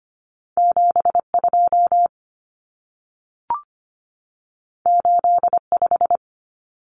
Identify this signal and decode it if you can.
Morse code, heard as a keyed tone, about 25 wpm, 700 Hz. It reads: 72 85
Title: Numbers - 2-Digit 25wpm